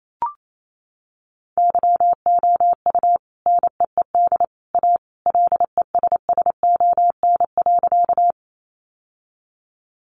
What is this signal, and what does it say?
Morse code, heard as a keyed tone, about 28 wpm, 700 Hz. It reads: YOU NEED A LESSON.
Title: Sentences from Top 1200 Words 28wpm